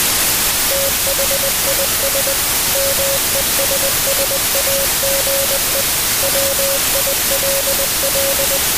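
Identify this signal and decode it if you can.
Morse code, heard as a keyed tone, about 20 wpm, 550 Hz, with radio noise. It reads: THIS MESSAGE WILL
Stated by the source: Morse code signal mixed with noise.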